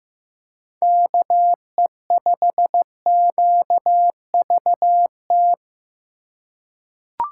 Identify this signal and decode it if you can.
Morse code, heard as a keyed tone, about 15 wpm, 700 Hz. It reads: KE5QVT
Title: USA Call Signs - 6 Characters - 3x Character Spacing - 15wpm